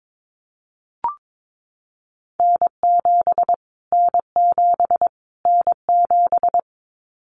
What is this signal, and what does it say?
Morse code, heard as a keyed tone, about 22 wpm, 700 Hz. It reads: N7 N7 N7
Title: Callsign Prefixes for Top 25 DXCC - Repeated 3x - 22wpm